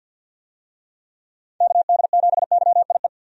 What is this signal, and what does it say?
Morse code, heard as a keyed tone, about 52 wpm, 700 Hz. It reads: KD7YIE
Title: USA Call Signs - 6 Characters - 50wpm